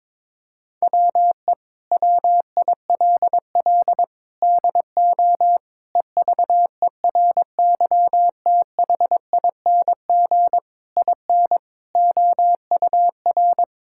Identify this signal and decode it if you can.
Morse code, heard as a keyed tone, about 22 wpm, 700 Hz. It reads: WE WILL DO EVERYTHING IN OUR
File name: practice / 22wpm / Sentences with 8-Word Phrases - 22wpm